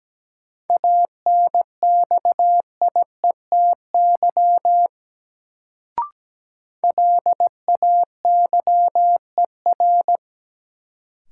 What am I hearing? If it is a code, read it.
Morse code, heard as a keyed tone, about 17 wpm, 700 Hz. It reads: ANXIETY LAYER
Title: Sentences from Top 1600 Words - New Word Drill - 17wpm